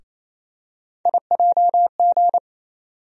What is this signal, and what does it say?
Morse code, heard as a keyed tone, about 28 wpm, 700 Hz. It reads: IJG
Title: USA Call Sign Postfixes - 3 Letters 28wpm